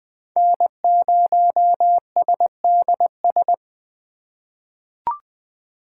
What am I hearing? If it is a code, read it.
Morse code, heard as a keyed tone, about 20 wpm, 700 Hz. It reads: N0SDS
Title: USA Call Signs - 5 Characters - 2x Character Spacing - 20wpm